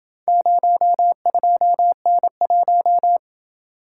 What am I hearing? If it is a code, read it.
Morse code, heard as a keyed tone, about 28 wpm, 700 Hz. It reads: 02N1